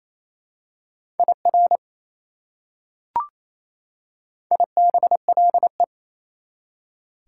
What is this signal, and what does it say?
Morse code, heard as a keyed tone, about 28 wpm, 700 Hz. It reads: IR IBLE
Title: Prefixes and Suffixes 28wpm